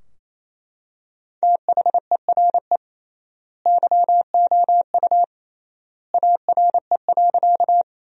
Morse code, heard as a keyed tone, about 27 wpm, 700 Hz. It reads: THERE YOU ARE.
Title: 3x Word-Spacing - Sentences from Top 100 Words - 28wpm